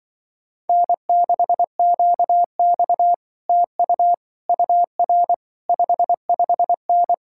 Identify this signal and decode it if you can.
Morse code, heard as a keyed tone, about 24 wpm, 700 Hz. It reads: N6QX TU UR 55N
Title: Speed-Racing Common QSO Phrases Encore - 20wpm